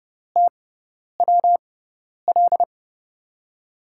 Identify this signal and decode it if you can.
Morse code, heard as a keyed tone, about 30 wpm, spaced as for 10 wpm, 700 Hz. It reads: TWL
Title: USA Call Signs - 6 Characters - 6x Character Spacing - 30wpm